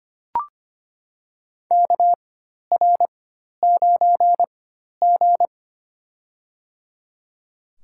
Morse code, heard as a keyed tone, about 25 wpm, spaced as for 12 wpm, 700 Hz. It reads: KR9G
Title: USA Call Signs - 4x Character Spacing - 25wpm